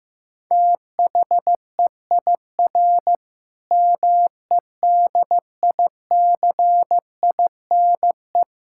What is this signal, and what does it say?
Morse code, heard as a keyed tone, about 15 wpm, 700 Hz. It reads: THEIR MEDICINE
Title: Sentences from Top 1000 Words - Review Part 6 - 15wpm